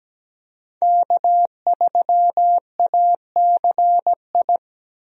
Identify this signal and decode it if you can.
Morse code, heard as a keyed tone, about 17 wpm, 700 Hz. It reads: K3ACI
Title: USA Call Signs - 5 Characters - Repeated 2x - 17wpm